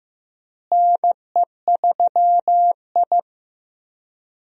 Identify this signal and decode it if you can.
Morse code, heard as a keyed tone, about 15 wpm, 700 Hz. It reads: NE3I
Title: USA Call Signs - 4 Characters - Repeated 3x - 15wpm